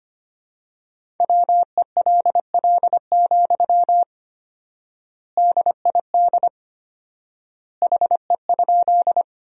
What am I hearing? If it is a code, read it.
Morse code, heard as a keyed tone, about 25 wpm, 700 Hz. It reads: WELL, DID HE?
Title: 4x Word-Spacing - Sentences from Top 200 Words - 25wpm